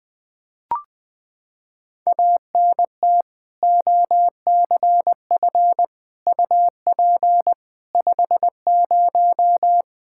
Morse code, heard as a keyed tone, about 20 wpm, 700 Hz. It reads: ANT OCF UP 50
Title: K1USN SST QSO Practice 20wpm